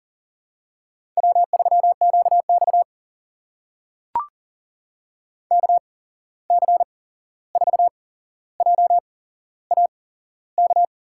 Morse code, heard as a keyed tone, about 40 wpm, spaced as for 15 wpm, 700 Hz. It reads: W3QX KC4JAK
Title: USA Call Signs - 8x Character Spacing - 40wpm